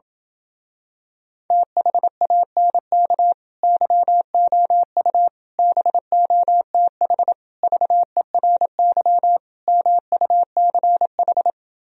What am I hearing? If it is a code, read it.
Morse code, heard as a keyed tone, about 26 wpm, 700 Hz. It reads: THANK YOU BOTH VERY MUCH